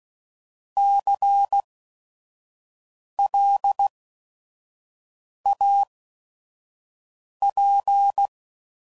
Morse code, heard as a keyed tone, about 16 wpm, spaced as for 5 wpm, 800 Hz. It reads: CLAP
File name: Prefixes / By First Letter / Cs